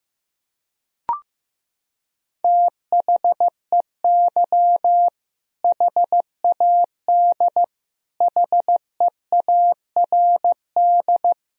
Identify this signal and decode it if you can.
Morse code, heard as a keyed tone, about 15 wpm, 700 Hz. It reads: THEY HAD HEARD
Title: Sentences from Top 400 Words and Abbreviations 15wpm